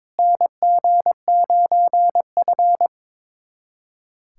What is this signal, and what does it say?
Morse code, heard as a keyed tone, about 22 wpm, 700 Hz. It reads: NG9F
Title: USA Callsign DE Callsign - 4 Characters - 22wpm